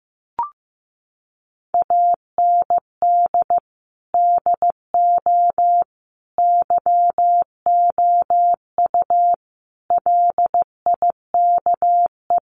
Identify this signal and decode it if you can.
Morse code, heard as a keyed tone, about 14 wpm, 700 Hz. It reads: AND DO YOU LIKE
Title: Sentences from Top 100 Words 15wpm